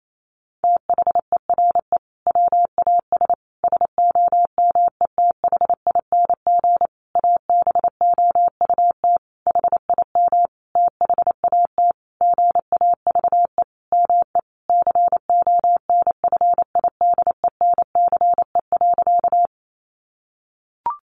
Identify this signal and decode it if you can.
Morse code, heard as a keyed tone, about 28 wpm, 700 Hz. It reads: THERE WAS SOMETHING ABOUT HIM THAT GAVE ME CONFIDENCE.
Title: Sentences from Top 1500 Words 28wpm